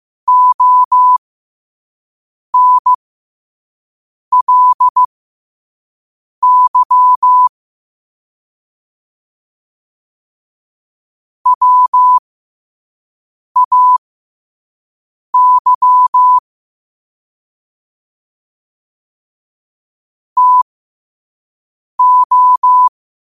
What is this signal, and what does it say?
Morse code, heard as a keyed tone, about 15 wpm, spaced as for 5 wpm, 1000 Hz. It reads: ONLY WAY TO
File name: qotdfiles / 5wpm